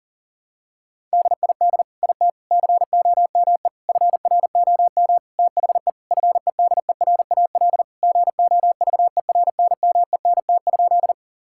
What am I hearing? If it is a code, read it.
Morse code, heard as a keyed tone, about 37 wpm, 700 Hz. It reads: DID IT COME FROM THE FEDERAL GOVERNMENT?
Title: Sentences from Top 1000 Words - Review Part 5 - 40wpm